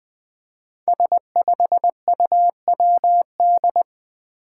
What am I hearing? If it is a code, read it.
Morse code, heard as a keyed tone, about 20 wpm, 700 Hz. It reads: S5UWD
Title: Five Challenging Letter-Number Combinations 20wpm